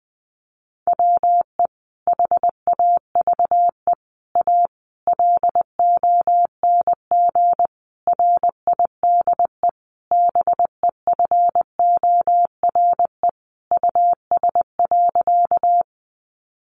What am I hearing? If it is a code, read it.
Morse code, heard as a keyed tone, about 20 wpm, 700 Hz. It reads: WE HAVE A LONG RIDE BEFORE US.
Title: Sentences from Top 1200 Words 20wpm